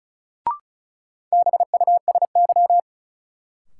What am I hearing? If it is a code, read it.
Morse code, heard as a keyed tone, about 36 wpm, 700 Hz. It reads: BUSY
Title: Sentences from Top 1000 Words - Review Part 2 - New Word Drill - 35wpm